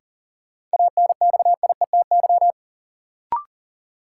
Morse code, heard as a keyed tone, about 40 wpm, 700 Hz. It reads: ANXIETY
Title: Sentences from Top 1600 Words - New Word Drill Alt - 40wpm